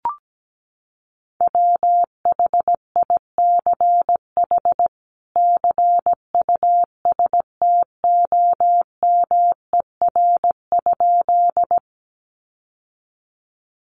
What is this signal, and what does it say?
Morse code, heard as a keyed tone, about 17 wpm, 700 Hz. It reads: WHICH CUSTOMER?
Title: Sentences from Top 1100 Words 17wpm